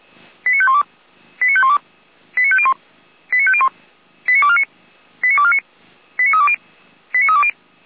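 Five-tone selective call (ZVEI), heard as quick runs of five tones, tones of about 70 ms. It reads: 89512 89512 89501 89501 89249 89249 89320 89320
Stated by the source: ZVEI five-tone selective call received via SDR, served by audio.f85360.feuerwehr-orlamuende.org (Feuerwehr Orlamünde fire brigade).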